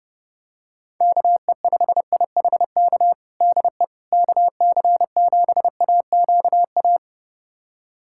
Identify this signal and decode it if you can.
Morse code, heard as a keyed tone, about 29 wpm, 700 Hz. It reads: KE5IHK DE KC7AQA
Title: USA Callsign DE Callsign - 6 Characters - 30wpm